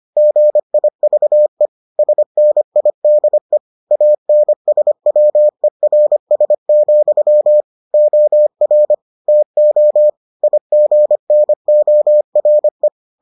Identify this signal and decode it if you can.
Morse code, heard as a keyed tone, about 25 wpm, 600 Hz. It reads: GIVE SNIDE ANSWERS, OR TO IGNORE